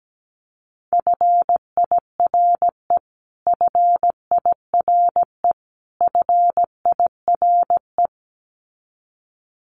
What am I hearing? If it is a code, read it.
Morse code, heard as a keyed tone, about 17 wpm, 700 Hz. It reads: FIRE FIRE FIRE
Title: Top 500 Words Repeated 17wpm